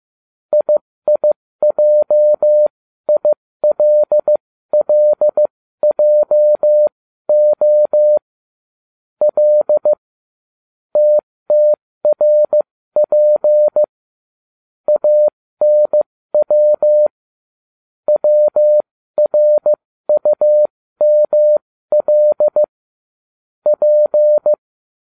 Morse code, heard as a keyed tone, about 15 wpm, spaced as for 13 wpm, 600 Hz. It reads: IIJILLJO L TTRP ANW WRUML P